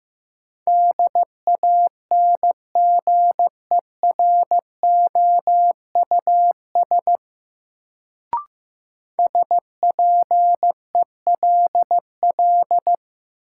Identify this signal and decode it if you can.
Morse code, heard as a keyed tone, about 15 wpm, 700 Hz. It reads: DANGEROUS SPELL
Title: Sentences from Top 1200 Words - New Word Drill Alt - 15wpm